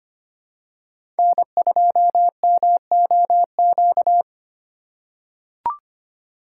Morse code, heard as a keyed tone, about 25 wpm, 700 Hz. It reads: N2MOQ
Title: USA Call Signs - 5 Characters - 3x Character Spacing - 25wpm